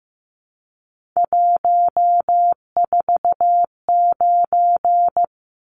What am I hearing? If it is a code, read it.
Morse code, heard as a keyed tone, about 15 wpm, 700 Hz. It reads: 149
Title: Numbers - 3-Digit 15wpm